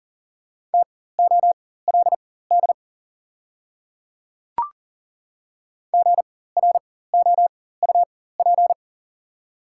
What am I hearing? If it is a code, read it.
Morse code, heard as a keyed tone, about 40 wpm, spaced as for 19 wpm, 690 Hz. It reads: TOLD GROUP
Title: 4x Character Spacing - Top 500 Words - 40wpm